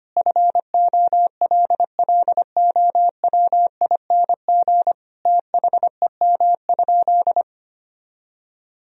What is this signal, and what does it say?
Morse code, heard as a keyed tone, about 25 wpm, 700 Hz. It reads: FOLLOWING THEM?
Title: Sentences from Top 1000 Words - Review Part 4 - 25wpm